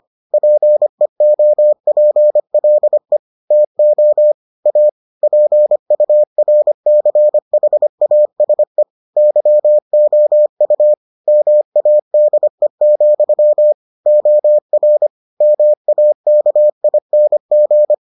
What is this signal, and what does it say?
Morse code, heard as a keyed tone, about 25 wpm, 600 Hz. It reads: PEOPLE TO A PURCHASE YOU MADE, OR MAKING